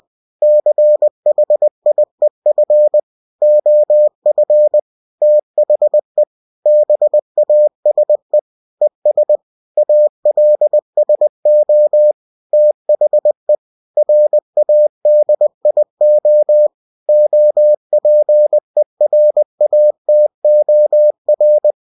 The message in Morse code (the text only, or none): CHIEF OF THE BASE ES ALSO THE RADIO OPERATOR